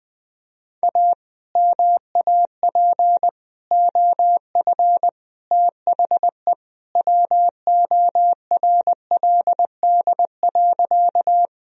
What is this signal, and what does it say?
Morse code, heard as a keyed tone, about 19 wpm, 700 Hz. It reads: A MAP OF THE WORLD.
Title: Sentences from Top 1000 Words - Review Part 2 - 20wpm